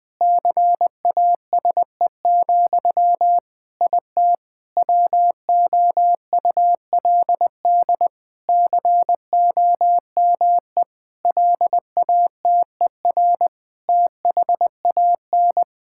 Morse code, heard as a keyed tone, about 20 wpm, 700 Hz. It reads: CASE, IT WOULD COME LATER THAN
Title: Sentences from Top 1600 Words 20wpm